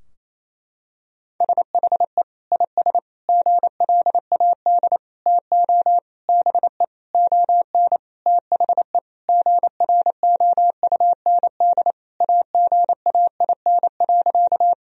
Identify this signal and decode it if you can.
Morse code, heard as a keyed tone, about 27 wpm, 700 Hz. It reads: SHE IS GLAD TO BE ON THE GROUND AGAIN.
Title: Sentences from Top 1000 Words - Review Part 2 - 28wpm